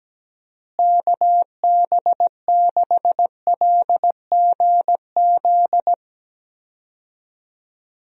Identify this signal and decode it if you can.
Morse code, heard as a keyed tone, about 17 wpm, 700 Hz. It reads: KB6LGZ
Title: USA Call Signs - 6 Characters - Repeated 2x - 17wpm